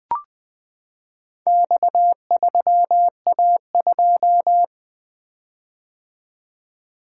Morse code, heard as a keyed tone, about 20 wpm, 700 Hz. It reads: X3A2
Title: Four Letter-Number Combinations - 20wpm